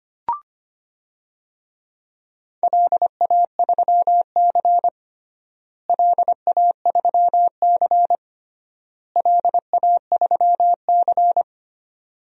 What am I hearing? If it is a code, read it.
Morse code, heard as a keyed tone, about 25 wpm, 700 Hz. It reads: LA3C LA3C LA3C
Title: DXCC-100 Call Signs - 4 Characters - Repeated 3x - 25wpm